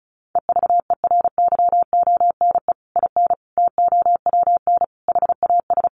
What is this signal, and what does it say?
Morse code, heard as a keyed tone, about 34 wpm, 700 Hz. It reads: EVERYONE IN TOWN HAS
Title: Sentences from Top 1500 Words 35wpm